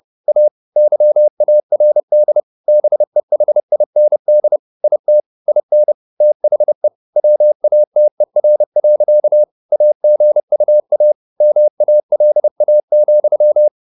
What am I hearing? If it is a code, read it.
Morse code, heard as a keyed tone, about 30 wpm, 600 Hz. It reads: A YARD BEHIND IT IN THE WATER. AGUA MALA,
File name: the old man and the sea / the old man and the sea 1 30wpm